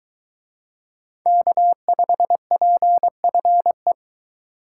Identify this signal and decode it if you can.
Morse code, heard as a keyed tone, about 22 wpm, 700 Hz. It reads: K5PFE